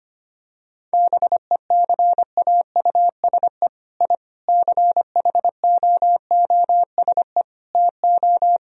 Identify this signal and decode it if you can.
Morse code, heard as a keyed tone, about 25 wpm, 700 Hz. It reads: BECAUSE I CHOOSE TO
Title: Sentences from Top 1000 Words - Review Part 4 - 25wpm